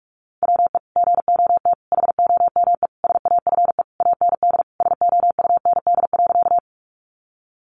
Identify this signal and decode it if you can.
Morse code, heard as a keyed tone, about 44 wpm, 700 Hz. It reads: WE GOT HOME SAFE AND SOUND.
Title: Sentences from Top 900 Words 45wpm